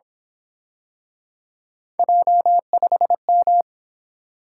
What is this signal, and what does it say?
Morse code, heard as a keyed tone, about 27 wpm, 700 Hz. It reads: J5M
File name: Speed-Racing - Three Letter-Number Combinations - 22wpm